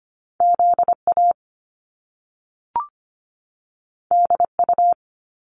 Morse code, heard as a keyed tone, about 25 wpm, 700 Hz. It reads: ZA DU
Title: Top Two Letter Combinations in Words 25wpm